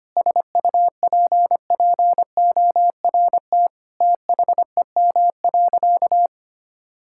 Morse code, heard as a keyed tone, about 25 wpm, 700 Hz. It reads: SUPPORT THEM.
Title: Sentences from Top 1000 Words - Review Part 5 - 25wpm